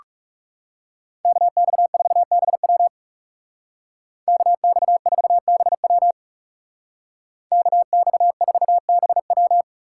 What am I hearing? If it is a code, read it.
Morse code, heard as a keyed tone, about 37 wpm, 700 Hz. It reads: KX4BW KX4BW KX4BW
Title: Speed Racing - USA Call Signs 30wpm